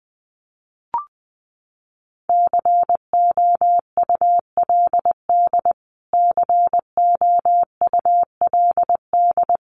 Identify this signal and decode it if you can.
Morse code, heard as a keyed tone, about 20 wpm, 700 Hz. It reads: COULD COULD
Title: Top 500 Words Repeated 20wpm